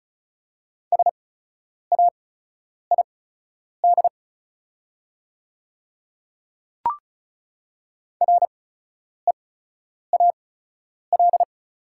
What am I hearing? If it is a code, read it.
Morse code, heard as a keyed tone, about 36 wpm, spaced as for 10 wpm, 700 Hz. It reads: SAID REAL
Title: 8x Character Spacing - Top 300 Words - 35wpm